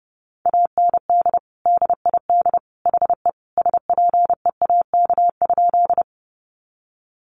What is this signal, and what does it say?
Morse code, heard as a keyed tone, about 30 wpm, 700 Hz. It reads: AND DID HE SPEAK?
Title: Sentences from Top 1100 Words 30wpm